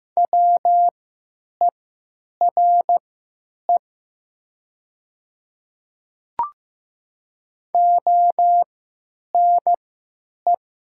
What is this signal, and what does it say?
Morse code, heard as a keyed tone, about 15 wpm, spaced as for 9 wpm, 700 Hz. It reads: WERE ONE
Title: 3x Character Spacing - Top 100 Words - 15wpm